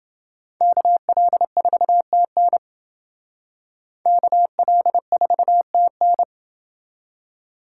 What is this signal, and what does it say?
Morse code, heard as a keyed tone, about 28 wpm, 700 Hz. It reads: KL4TN KL4TN